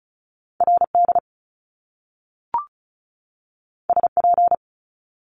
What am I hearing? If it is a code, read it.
Morse code, heard as a keyed tone, about 36 wpm, 700 Hz. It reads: RD SP
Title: Top Two Letter Combinations in Words 35wpm